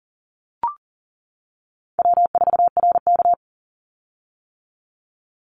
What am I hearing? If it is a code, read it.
Morse code, heard as a keyed tone, about 40 wpm, 700 Hz. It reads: W4RK
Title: Easy Call Signs - 40wpm